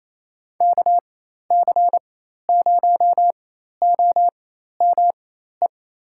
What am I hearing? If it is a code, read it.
Morse code, heard as a keyed tone, about 27 wpm, spaced as for 13 wpm, 700 Hz. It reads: KC0OME
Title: USA Call Signs - 6 Characters - 4x Character Spacing - 28wpm